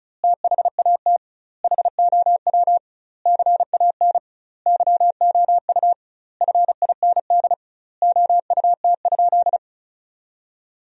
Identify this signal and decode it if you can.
Morse code, heard as a keyed tone, about 34 wpm, 700 Hz. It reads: THAT HOW CAN YOU FIND OUT?
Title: 2x Word-Spacing - Sentences from Top 200 Words - 35wpm